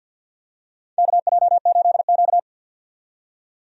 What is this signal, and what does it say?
Morse code, heard as a keyed tone, about 50 wpm, 700 Hz. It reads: KJ8Q
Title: USA Call Signs - 4 Characters - 50wpm